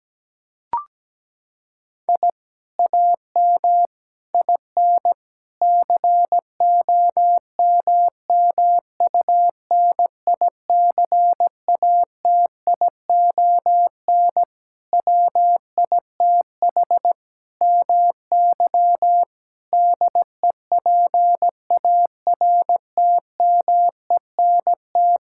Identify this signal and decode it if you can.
Morse code, heard as a keyed tone, about 17 wpm, 700 Hz. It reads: I AM IN COMMUNICATION WITH MY DEPARTMENT
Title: Sentences from Top 1000 Words - Review Part 6 - 17wpm